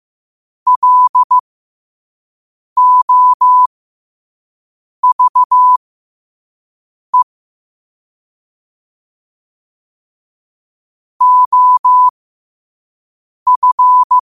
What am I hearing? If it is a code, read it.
Morse code, heard as a keyed tone, about 15 wpm, spaced as for 5 wpm, 1000 Hz. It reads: LOVE OF
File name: qotdfiles / 5wpm